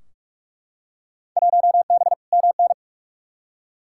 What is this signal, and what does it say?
Morse code, heard as a keyed tone, about 44 wpm, 700 Hz. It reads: 1B MN
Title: Field Day Exchanges Encore - 45wpm